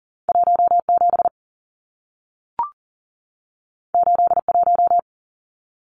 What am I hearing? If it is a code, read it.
Morse code, heard as a keyed tone, about 40 wpm, 700 Hz. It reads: 17 81
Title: Numbers - 2-Digit 40wpm